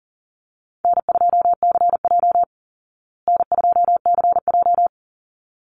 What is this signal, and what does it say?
Morse code, heard as a keyed tone, about 40 wpm, 700 Hz. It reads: N2CJ N2CJ
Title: Call Signs Repeated 3x - 40wpm